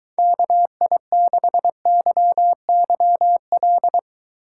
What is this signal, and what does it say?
Morse code, heard as a keyed tone, about 23 wpm, 700 Hz. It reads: KI6YYL